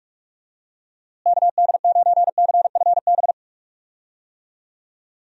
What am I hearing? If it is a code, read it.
Morse code, heard as a keyed tone, about 44 wpm, 700 Hz. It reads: KD9CFB
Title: USA Call Signs - 6 Characters - Repeated 2x - 45wpm